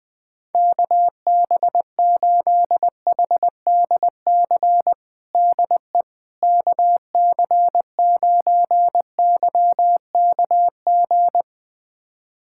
Morse code, heard as a keyed tone, about 20 wpm, 700 Hz. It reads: KB8HDC DE KC9YKG
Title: USA Callsign DE Callsign - 6 Characters - 20wpm